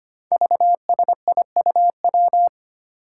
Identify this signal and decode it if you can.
Morse code, heard as a keyed tone, about 25 wpm, 700 Hz. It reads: VSIUW
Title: Five Challenging Letter-Number Combinations 25wpm